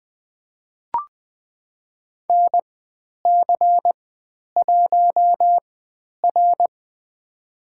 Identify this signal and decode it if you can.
Morse code, heard as a keyed tone, about 20 wpm, spaced as for 10 wpm, 700 Hz. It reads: NC1R
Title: 102 Lesson 17g - Letter u - Call Signs 20wpm